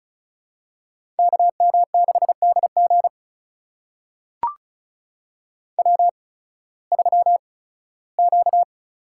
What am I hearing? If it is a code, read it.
Morse code, heard as a keyed tone, about 34 wpm, spaced as for 18 wpm, 700 Hz. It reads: KM6DG W3Q
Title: USA Call Signs - 8x Character Spacing - 35wpm